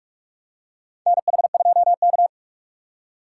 Morse code, heard as a keyed tone, about 47 wpm, 700 Hz. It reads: NH2K
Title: USA Call Signs - 4 Characters - 2x Character Spacing - 45wpm